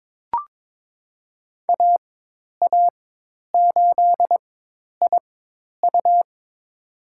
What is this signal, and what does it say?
Morse code, heard as a keyed tone, about 22 wpm, spaced as for 10 wpm, 700 Hz. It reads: AA8IU
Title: USA Call Signs - 5 Characters - 4x Character Spacing - 22wpm